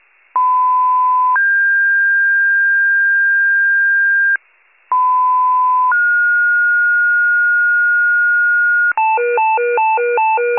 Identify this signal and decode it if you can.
Two-tone page, heard as a one-second tone, then a three-second tone, about 1010 then 1600 Hz, about 1010 then 1430 Hz, with noise. It is heard from 0.3 to 8.9 s.
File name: pager / 94 RESCUE DUTY / wcalert